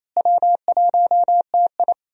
Morse code, heard as a keyed tone, about 28 wpm, 700 Hz. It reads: W1TI
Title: USA Call Signs - 4 Characters - Repeated 2x - 28wpm